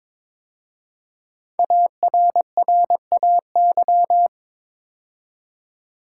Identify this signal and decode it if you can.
Morse code, heard as a keyed tone, about 23 wpm, 700 Hz. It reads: ARRAY
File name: 5 Letter Words - First 3 Letters Given - 22wpm